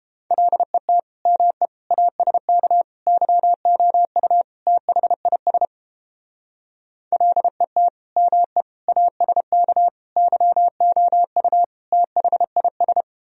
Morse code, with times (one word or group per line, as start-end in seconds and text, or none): 0.3-1.0 LET
1.3-1.7 ME
1.9-2.8 ASK
3.1-4.4 YOU
4.7-5.7 THIS
7.1-7.9 LET
8.2-8.6 ME
8.9-9.9 ASK
10.2-11.6 YOU
11.9-13.0 THIS